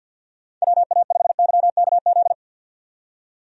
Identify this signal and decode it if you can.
Morse code, heard as a keyed tone, about 47 wpm, 700 Hz. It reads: WA5YKZ
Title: USA Call Signs - 6 Characters - 50wpm